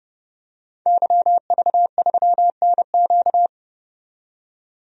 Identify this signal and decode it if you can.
Morse code, heard as a keyed tone, about 30 wpm, 700 Hz. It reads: YV3NQ
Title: DXCC-100 Call Signs - 5 Characters - 30wpm